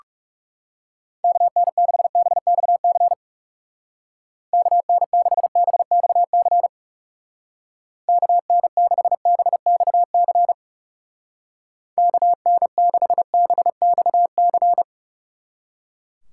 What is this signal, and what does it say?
Morse code, heard as a keyed tone, about 37 wpm, 700 Hz. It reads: KN6BXC KN6BXC KN6BXC KN6BXC
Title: Speed Racing - USA 2x3 Call Signs 30wpm